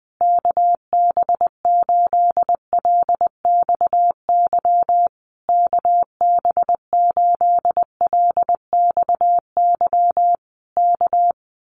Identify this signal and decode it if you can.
Morse code, heard as a keyed tone, about 20 wpm, 700 Hz. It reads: KB8LXY KB8LXY K
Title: Common QSO Phrases Encore 20wpm